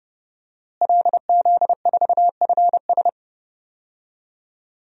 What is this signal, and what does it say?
Morse code, heard as a keyed tone, about 31 wpm, 700 Hz. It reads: LZ4FS
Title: DXCC-100 Call Signs - 5 Characters - Repeated 3x - 30wpm